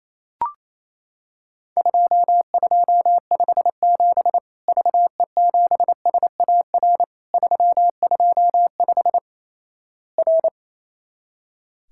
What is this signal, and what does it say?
Morse code, heard as a keyed tone, about 28 wpm, 700 Hz. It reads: 2257 VE7SAR 325 R
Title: Worked All Europe DX Contest - QTC Exchange - 28wpm